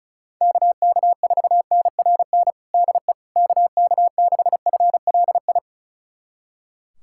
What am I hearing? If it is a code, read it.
Morse code, heard as a keyed tone, about 34 wpm, 700 Hz. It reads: KK4NRN DE KK6FLI
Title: USA Callsign DE Callsign - 6 Characters - 35wpm